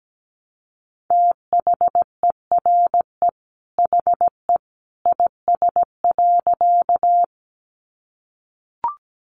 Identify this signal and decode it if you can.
Morse code, heard as a keyed tone, about 17 wpm, 700 Hz. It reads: THERE HE IS.
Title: A Sound Of Thunder 17wpm